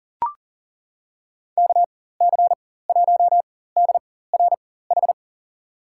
Morse code, heard as a keyed tone, about 40 wpm, spaced as for 19 wpm, 700 Hz. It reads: KC1DRH
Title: USA Call Signs - 6 Characters - 4x Character Spacing - 40wpm